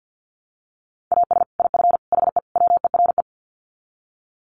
Morse code, heard as a keyed tone, about 50 wpm, 700 Hz. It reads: AS IF HE WERE
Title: Sets of 4 Words 50wpm